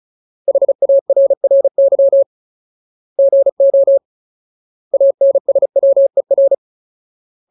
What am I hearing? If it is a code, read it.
Morse code, heard as a keyed tone, about 34 wpm, 550 Hz. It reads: HARRY GO ANSWER
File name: Ultimate Word List / 35WPM Ultimate